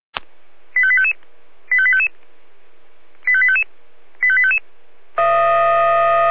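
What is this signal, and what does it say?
Five-tone selective call (ZVEI), heard as quick runs of five tones, tones of about 70 ms. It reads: 85855 85855 85855 85855
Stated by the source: ZVEI five-tone selective call received by SDR, code 85320.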